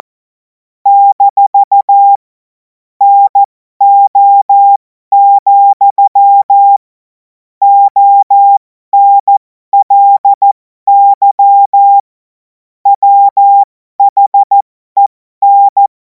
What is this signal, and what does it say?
Morse code, heard as a keyed tone, about 14 wpm, spaced as for 12 wpm, 800 Hz. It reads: - NO, ONLY WHEN
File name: Morsekurs / Witze